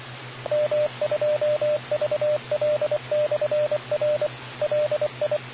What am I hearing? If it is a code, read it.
Morse code, heard as a keyed tone, about 24 wpm, 600 Hz, with radio noise. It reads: M2VL/R LI